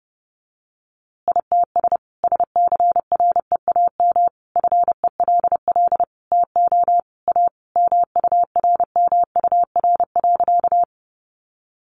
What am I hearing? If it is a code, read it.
Morse code, heard as a keyed tone, about 30 wpm, 700 Hz. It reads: ITS SCREAM FELL TO A MURMUR.
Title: A Sound Of Thunder 30wpm